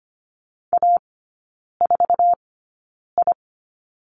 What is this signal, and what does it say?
Morse code, heard as a keyed tone, about 25 wpm, spaced as for 9 wpm, 700 Hz. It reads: A4I
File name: course / 25:10wpm / 093 Lesson 16e - Letter c - Three Characters 25wpm